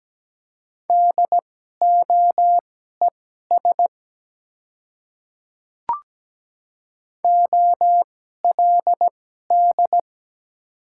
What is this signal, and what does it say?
Morse code, heard as a keyed tone, about 17 wpm, spaced as for 12 wpm, 700 Hz. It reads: DOES OLD
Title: 2x Character Spacing - Top 200 Words - 17wpm